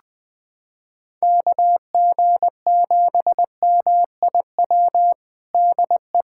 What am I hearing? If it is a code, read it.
Morse code, heard as a keyed tone, about 20 wpm, 700 Hz. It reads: KG7MIW DE
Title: USA Callsign DE Callsign 20wpm